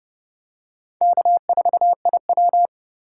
Morse code, heard as a keyed tone, about 26 wpm, 700 Hz. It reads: K4IW